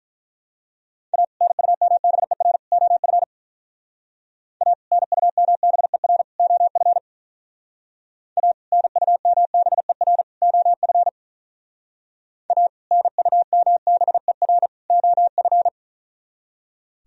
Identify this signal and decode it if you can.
Morse code, heard as a keyed tone, about 43 wpm, 700 Hz. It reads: A NUMBER OF A NUMBER OF A NUMBER OF A NUMBER OF
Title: Speed-Racing - Sets of 3 Words 35wpm